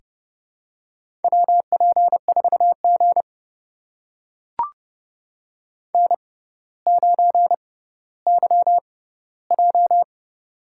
Morse code, heard as a keyed tone, about 30 wpm, spaced as for 15 wpm, 700 Hz. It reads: WP4G N9YJ
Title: USA Call Signs - 4 Characters - 6x Character Spacing - 30wpm